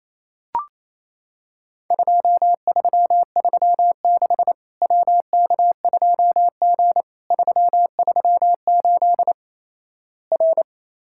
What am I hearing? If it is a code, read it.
Morse code, heard as a keyed tone, about 28 wpm, 700 Hz. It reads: 2336 WK2G 338 R
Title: Worked All Europe DX Contest - QTC Exchange - 28wpm